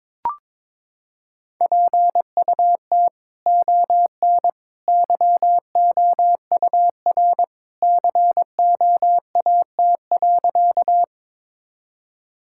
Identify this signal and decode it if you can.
Morse code, heard as a keyed tone, about 22 wpm, 700 Hz. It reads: PUT ON YOUR COAT.
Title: Sentences from Top 1600 Words 22wpm